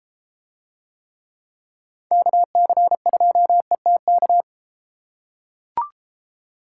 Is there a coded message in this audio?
Yes: Morse code from 2.1 to 4.4 s, heard as a keyed tone, about 32 wpm, 700 Hz. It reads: KC2ETK